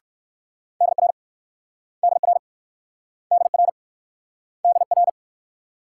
Morse code, heard as a keyed tone, about 54 wpm, 690 Hz. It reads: DR DR DR DR
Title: Speed-Racing Multi-letter Phonemes - 45wpm